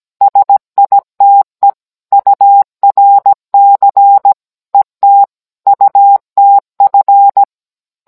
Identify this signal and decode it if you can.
Morse code, heard as a keyed tone, about 17 wpm, 800 Hz. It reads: SITE URC ET UTF